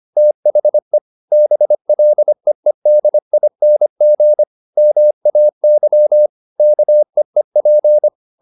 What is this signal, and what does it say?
Morse code, heard as a keyed tone, about 25 wpm, 600 Hz. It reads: THE BLEEDING MAY KEEP